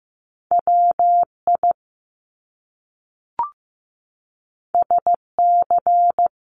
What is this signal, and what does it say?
Morse code, heard as a keyed tone, about 15 wpm, 700 Hz. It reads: WI SC
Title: US States Repeated 3x 15wpm